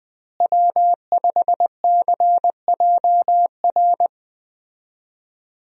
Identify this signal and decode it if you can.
Morse code, heard as a keyed tone, about 20 wpm, 700 Hz. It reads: W5CJR
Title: USA Call Signs - 5 Characters - Repeated 3x - 20wpm